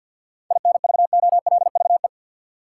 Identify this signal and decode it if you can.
Morse code, heard as a keyed tone, about 52 wpm, 700 Hz. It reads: INVOLVE
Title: Sentences from Top 1200 Words - New Word Drill - 50wpm